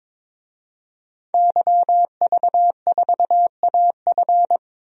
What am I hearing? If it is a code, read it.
Morse code, heard as a keyed tone, about 22 wpm, 700 Hz. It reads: YV4AF
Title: DXCC-100 Call Signs - 5 Characters - 22wpm